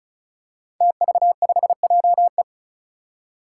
Morse code, heard as a keyed tone, about 34 wpm, 700 Hz. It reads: TV5JE
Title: Five Challenging Letter-Number Combinations 35wpm